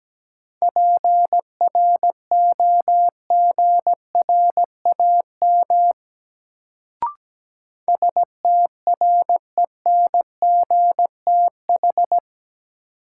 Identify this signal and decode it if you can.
Morse code, heard as a keyed tone, about 17 wpm, 700 Hz. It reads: PROGRAM STRENGTH
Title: Sentences from Top 1000 Words - Review Part 4 - New Word Drill - 17wpm